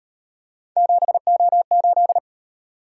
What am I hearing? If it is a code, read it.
Morse code, heard as a keyed tone, about 38 wpm, 700 Hz. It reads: 7O8